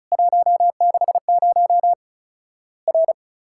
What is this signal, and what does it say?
Morse code, heard as a keyed tone, about 36 wpm, 700 Hz. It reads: 160 R
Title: Worked All Europe DX Contest - QTC Exchange - 35wpm